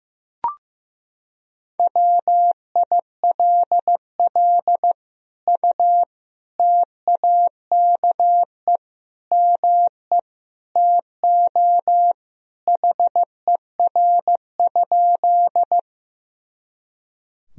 Morse code, heard as a keyed tone, about 15 wpm, 700 Hz. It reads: WILL U TAKE ME TO HER?
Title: Sentences from Top 200 Words and Abbreviations 15wpm